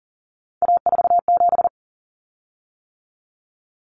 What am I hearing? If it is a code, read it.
Morse code, heard as a keyed tone, about 40 wpm, 700 Hz. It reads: A47
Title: Common Cut Numbers - 3-Digit 40wpm